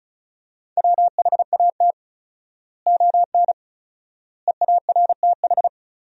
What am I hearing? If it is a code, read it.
Morse code, heard as a keyed tone, about 34 wpm, 700 Hz. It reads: WHAT ON EARTH
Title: 4x Word-Spacing - Sentences from Top 200 Words - 35wpm